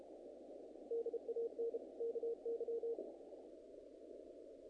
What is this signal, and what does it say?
Morse code, heard as a keyed tone, about 29 wpm, 470 Hz. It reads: DAN KY
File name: contests / naqp-cw-k5tr-op-k5pi- -1900